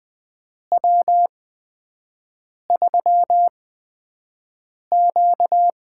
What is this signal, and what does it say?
Morse code, heard as a keyed tone, about 20 wpm, spaced as for 5 wpm, 700 Hz. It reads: W3Q
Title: USA Call Signs - 4 Characters - 8x Character Spacing - 20wpm